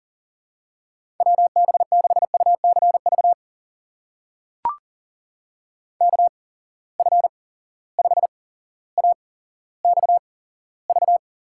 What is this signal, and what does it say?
Morse code, heard as a keyed tone, about 39 wpm, spaced as for 17 wpm, 700 Hz. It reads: WB6UCV KF5AXV
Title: USA Call Signs - 6 Characters - 8x Character Spacing - 40wpm